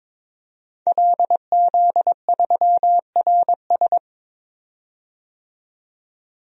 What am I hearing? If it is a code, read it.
Morse code, heard as a keyed tone, about 22 wpm, 700 Hz. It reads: LZ3RS
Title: DXCC-100 Call Signs - 5 Characters - 22wpm